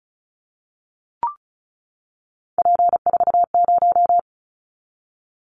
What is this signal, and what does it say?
Morse code, heard as a keyed tone, about 36 wpm, 700 Hz. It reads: P40
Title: Callsign Prefixes for Top 100 DXCC - 35wpm